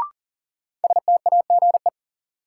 Morse code, heard as a keyed tone, about 40 wpm, 700 Hz. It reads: STAGE
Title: Sentences from Top 1200 Words - New Word Drill - 40wpm